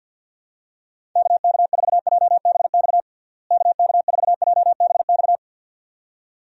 Morse code, heard as a keyed tone, about 46 wpm, 700 Hz. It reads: KK4JBX KK4JBX
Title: USA Call Signs - 6 Characters - Repeated 2x - 50wpm